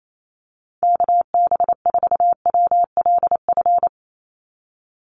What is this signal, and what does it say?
Morse code, heard as a keyed tone, about 28 wpm, 700 Hz. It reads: KB4WLF
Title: Call Signs - 28wpm